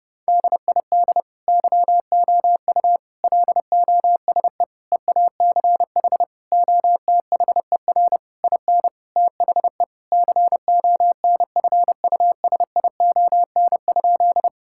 Morse code, heard as a keyed tone, about 30 wpm, 700 Hz. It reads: DID YOU LOSE EACH OTHER IN THE CONFUSION?
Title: Sentences from Top 1600 Words 30wpm